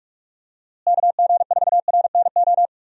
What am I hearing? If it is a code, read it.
Morse code, heard as a keyed tone, about 41 wpm, 700 Hz. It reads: KG4RNO
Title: USA Call Signs - 6 Characters - Repeated 3x - 45wpm